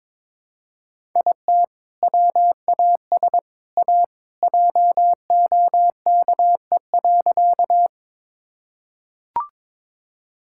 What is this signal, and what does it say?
Morse code, heard as a keyed tone, about 22 wpm, 700 Hz. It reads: IT WAS A JOKE.
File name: practice / 22wpm / Sentences with 3-Word Phrases - 22wpm